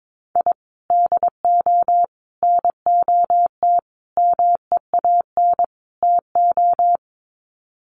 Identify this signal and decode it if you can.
Morse code, heard as a keyed tone, about 22 wpm, 700 Hz. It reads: I DO NOT MEAN TO
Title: Sets of 5 Words 22wpm